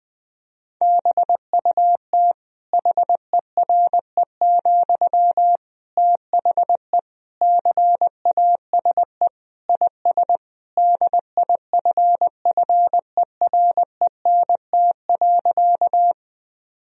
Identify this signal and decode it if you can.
Morse code, heard as a keyed tone, about 21 wpm, 700 Hz. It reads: BUT HERE, THE CASE IS DIFFERENT.
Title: Sentences from Top 1000 Words - Review Part 3 - 20wpm